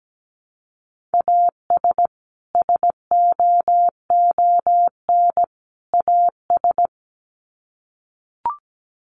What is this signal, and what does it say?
Morse code, heard as a keyed tone, about 17 wpm, 700 Hz. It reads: AS SOON AS
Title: Sets of 3 Words 17wpm